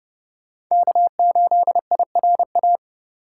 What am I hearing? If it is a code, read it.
Morse code, heard as a keyed tone, about 36 wpm, 700 Hz. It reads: K8IRA
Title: Speed Racing - USA 1x3 Call Signs 30wpm